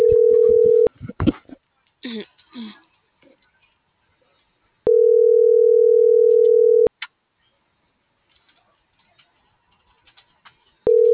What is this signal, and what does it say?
Ringback tone (North America), 440 + 480 Hz.